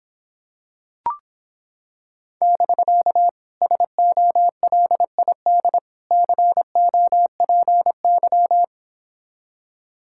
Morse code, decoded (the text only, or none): <BK> SOLID COPY